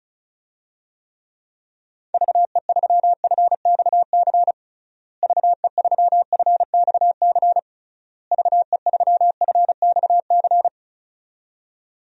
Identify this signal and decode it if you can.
Morse code, heard as a keyed tone, about 36 wpm, 700 Hz. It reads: VE3FXC VE3FXC VE3FXC
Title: DXCC-100 Call Signs - 6 Characters - Repeated 3x - 35wpm